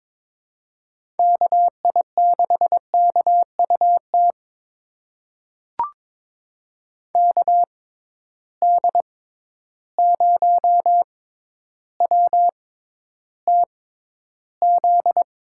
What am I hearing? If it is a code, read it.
Morse code, heard as a keyed tone, about 22 wpm, spaced as for 11 wpm, 700 Hz. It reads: KI6KUT KD0WTZ
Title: USA Call Signs - 6 Characters - 6x Character Spacing - 22wpm